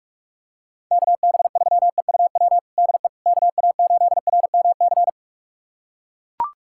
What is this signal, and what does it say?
Morse code, heard as a keyed tone, about 41 wpm, 700 Hz. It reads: KB3EUW DE KA8RMC
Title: USA Callsign DE Callsign - 6 Characters - 45wpm